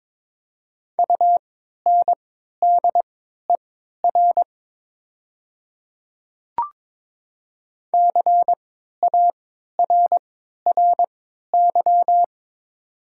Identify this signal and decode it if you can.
Morse code, heard as a keyed tone, about 22 wpm, spaced as for 12 wpm, 700 Hz. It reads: UNDER CARRY
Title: 3x Character Spacing - Top 300 Words - 22wpm